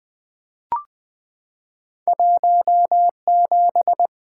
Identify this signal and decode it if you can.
Morse code, heard as a keyed tone, about 20 wpm, 700 Hz. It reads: 17
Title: Two Letter-Number Combinations - 20wpm